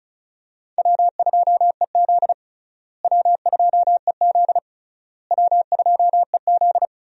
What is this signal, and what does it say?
Morse code, heard as a keyed tone, about 36 wpm, 700 Hz. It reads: W2EZ W2EZ W2EZ
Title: USA Call Signs - 4 Characters - Repeated 3x - 35wpm